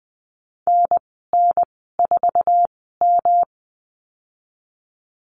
Morse code, heard as a keyed tone, about 20 wpm, spaced as for 14 wpm, 700 Hz. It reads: NN4M
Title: 154 Lesson 25g - Letter p - Call Signs 20wpm